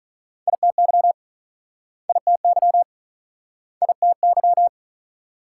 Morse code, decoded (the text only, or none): ITY ITY ITY